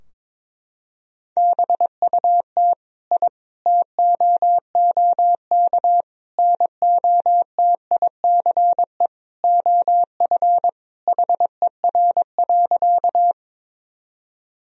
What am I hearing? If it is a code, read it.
Morse code, heard as a keyed tone, about 22 wpm, 700 Hz. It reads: BUT I TOOK NOTICE OF HER.
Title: Sentences from Top 1000 Words - Review Part 2 - 22wpm